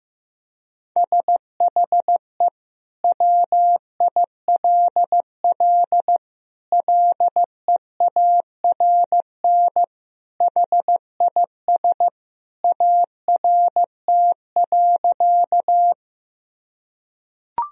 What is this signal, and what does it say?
Morse code, heard as a keyed tone, about 15 wpm, 700 Hz. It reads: SHE WILL LEARN HIS ART.
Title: Sentences from Top 1000 Words - Review Part 3 - 15wpm